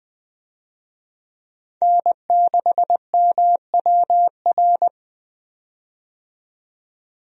Morse code, heard as a keyed tone, about 20 wpm, 700 Hz. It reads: N6MWR
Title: USA Call Signs - 5 Characters - 20wpm